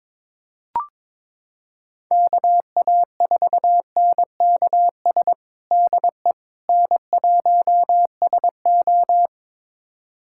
Morse code, heard as a keyed tone, about 22 wpm, 700 Hz. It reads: KA4NKS DE N1SO
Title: USA Callsign DE Callsign 22wpm